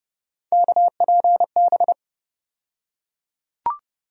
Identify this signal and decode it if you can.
Morse code, heard as a keyed tone, about 29 wpm, 700 Hz. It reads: KPB
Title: USA Call Sign Postfixes - 3 Letters 30wpm